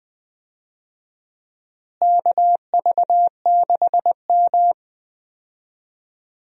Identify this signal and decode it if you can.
Morse code, heard as a keyed tone, about 20 wpm, 700 Hz. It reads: KV6M
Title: USA Call Signs - 4 Characters - Repeated 2x - 20wpm